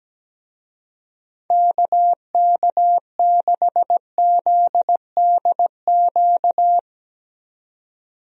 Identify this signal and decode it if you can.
Morse code, heard as a keyed tone, about 17 wpm, 700 Hz. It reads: KK6ZDQ
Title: USA Call Signs - 6 Characters - 17wpm